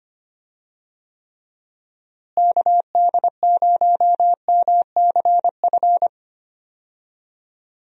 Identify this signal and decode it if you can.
Morse code, heard as a keyed tone, about 25 wpm, 700 Hz. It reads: KD0MCF
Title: USA Call Signs - 6 Characters - Repeated 2x - 25wpm